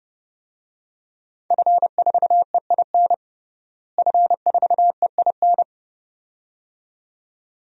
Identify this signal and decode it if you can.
Morse code, heard as a keyed tone, about 30 wpm, 700 Hz. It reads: F4EIN F4EIN
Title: DXCC-100 Call Signs - 5 Characters - Repeated 2x - 30wpm